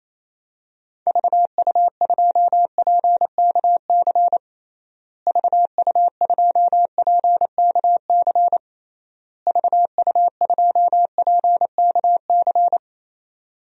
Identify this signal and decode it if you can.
Morse code, heard as a keyed tone, about 28 wpm, 700 Hz. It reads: VU2PKC VU2PKC VU2PKC
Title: DXCC-100 Call Signs - 6 Characters - Repeated 3x - 28wpm